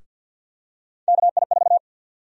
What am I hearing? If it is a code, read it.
Morse code, heard as a keyed tone, about 52 wpm, 700 Hz. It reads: KI4
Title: USA Call Sign Prefixes - 50wpm